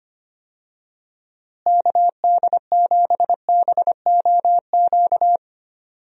Morse code, heard as a keyed tone, about 25 wpm, 700 Hz. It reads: KD7BOQ
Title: USA Call Signs - 6 Characters - Repeated 3x - 25wpm